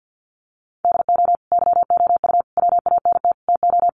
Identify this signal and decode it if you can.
Morse code, heard as a keyed tone, about 50 wpm, 700 Hz. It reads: DO YOU WANT TO
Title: Sentences from Top 1200 Words 50wpm